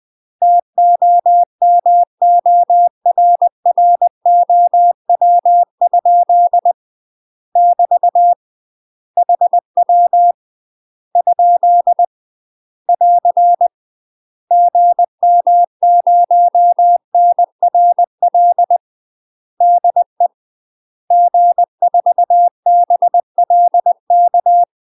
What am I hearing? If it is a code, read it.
Morse code, heard as a keyed tone, about 20 wpm, 700 Hz. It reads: TOMORROW? <BT> HW ? <AR> GM0NRL DE G4BLK
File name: UK MASTER QSOs TRAINING / 20WPM / UK MASTER QSOs 0059